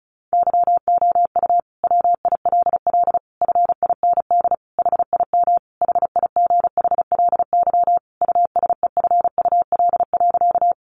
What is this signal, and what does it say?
Morse code, heard as a keyed tone, about 34 wpm, 700 Hz. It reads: YOU WILL FIND HIM HIGHLY USEFUL.
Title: Sentences from Top 1100 Words 35wpm